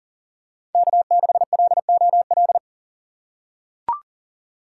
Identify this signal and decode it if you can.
Morse code, heard as a keyed tone, about 40 wpm, 700 Hz. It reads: K6LOL
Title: USA Call Signs - 2x Character Spacing - 40wpm